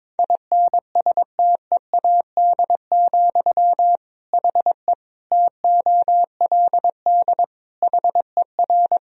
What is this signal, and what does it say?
Morse code, heard as a keyed tone, about 22 wpm, 700 Hz. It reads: INSTEAD, HE TOLD HER
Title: Sentences from Top 1000 Words - Review Part 4 - 22wpm